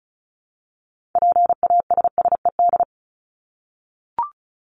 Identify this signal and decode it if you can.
Morse code, heard as a keyed tone, about 34 wpm, 700 Hz. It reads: PASSED
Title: ED Suffix - Top 100 Words with Base Words 35wpm